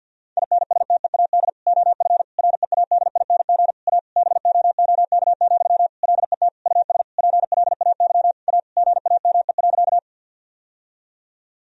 Morse code, heard as a keyed tone, about 50 wpm, 700 Hz. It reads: INSTEAD OF READING A BOOK, LET US PLAY A GAME.
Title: Sentences from Top 1000 Words - Review Part 4 - 50wpm